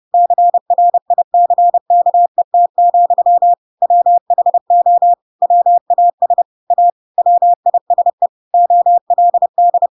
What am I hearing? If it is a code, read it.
Morse code, heard as a keyed tone, about 30 wpm, 700 Hz. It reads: CRICKET, WHO WAS A WISE OLD
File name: Pinocchio / Pinocchio ch4 30 30 ik1hge